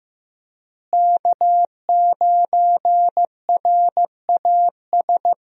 Morse code, heard as a keyed tone, about 15 wpm, 700 Hz. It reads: K9RAS
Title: USA Call Signs - 5 Characters - Repeated 2x - 15wpm